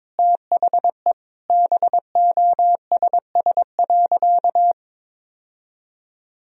Morse code, heard as a keyed tone, about 22 wpm, 700 Hz. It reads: THE BOSS.
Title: Sentences from Top 1000 Words - Review Part 4 - 22wpm